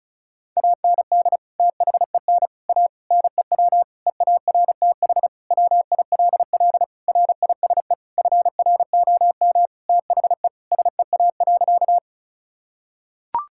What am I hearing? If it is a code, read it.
Morse code, heard as a keyed tone, about 36 wpm, 700 Hz. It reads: AND THEN A NEW EARTH WILL RISE FROM THE SEA.
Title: Sentences from Top 1000 Words - Review Part 6 - 35wpm